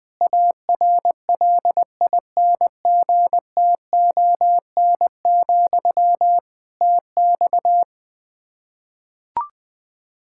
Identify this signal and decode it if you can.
Morse code, heard as a keyed tone, about 21 wpm, 700 Hz. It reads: ARLINGTON, TX
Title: Practice QSO - Exchange 1 - Beginning - 20wpm